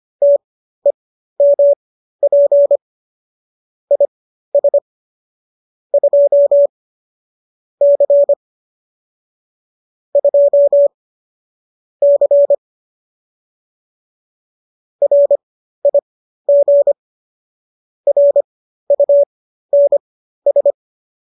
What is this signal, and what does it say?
Morse code, heard as a keyed tone, about 25 wpm, spaced as for 13 wpm, 570 Hz. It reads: TEMP IS 2 C 2 C RIG RUNS